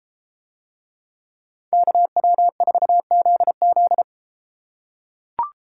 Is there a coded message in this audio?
Yes: Morse code from 1.7 to 4.0 s, heard as a keyed tone, about 32 wpm, 700 Hz. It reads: KW4ZZ